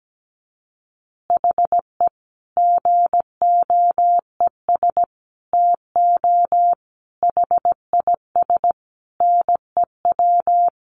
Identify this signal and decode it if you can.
Morse code, heard as a keyed tone, about 17 wpm, 700 Hz. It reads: HE GOES TO HIS NEW
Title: Easy Phrases 17wpm